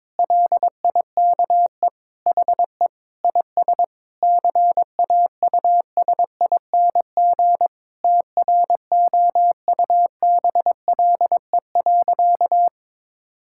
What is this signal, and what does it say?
Morse code, heard as a keyed tone, about 22 wpm, 700 Hz. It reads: LIKE HE IS CAUSING TROUBLE.
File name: Sentences with 9-Word Phrases - 22wpm